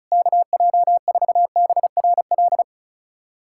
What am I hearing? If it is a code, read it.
Morse code, heard as a keyed tone, about 34 wpm, 700 Hz. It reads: KJ4BRL
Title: USA Callsign DE Callsign 35wpm